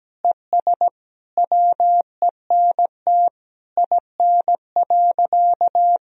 Morse code, heard as a keyed tone, about 17 wpm, 700 Hz. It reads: ES WENT IN.
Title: Sentences from Top 300 Words and Abbreviations 17wpm